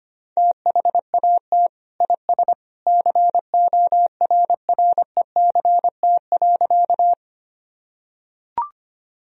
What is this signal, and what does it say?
Morse code, heard as a keyed tone, about 25 wpm, 700 Hz. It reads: THAT IS CORRECT.
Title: Sentences from Top 1000 Words - Review Part 2 - 25wpm